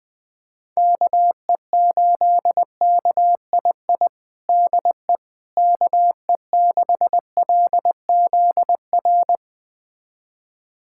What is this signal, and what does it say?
Morse code, heard as a keyed tone, about 20 wpm, 700 Hz. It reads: KE8KII DE KE6LZR
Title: USA Callsign DE Callsign - 6 Characters - 20wpm